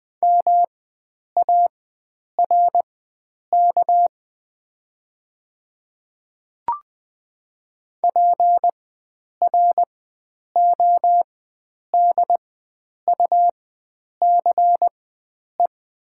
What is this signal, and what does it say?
Morse code, heard as a keyed tone, about 20 wpm, spaced as for 9 wpm, 700 Hz. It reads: MARK PRODUCE
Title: 4x Character Spacing - Top 500 Words - 20wpm